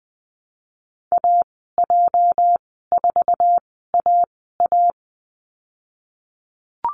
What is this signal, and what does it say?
Morse code, heard as a keyed tone, about 20 wpm, spaced as for 14 wpm, 700 Hz. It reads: AJ4AA
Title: 231 Lesson 37g - Letter x - Call Signs 20wpm